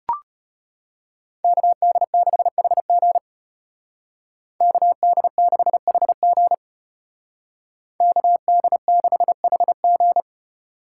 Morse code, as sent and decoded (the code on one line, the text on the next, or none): -.- -.. -.... .... --. / -.- -.. -.... .... --. / -.- -.. -.... .... --.
KD6HG KD6HG KD6HG